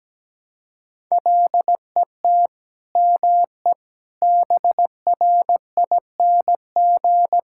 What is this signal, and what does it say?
Morse code, heard as a keyed tone, about 17 wpm, 700 Hz. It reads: LET ME BRING
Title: Sentences from Top 1000 Words - Review Part 6 - 17wpm